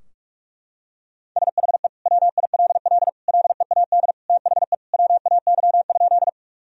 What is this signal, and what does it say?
Morse code, heard as a keyed tone, about 46 wpm, 700 Hz. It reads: SHE WILL LEAD THE WAY?
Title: Sentences from Top 1000 Words - Review Part 2 - 45wpm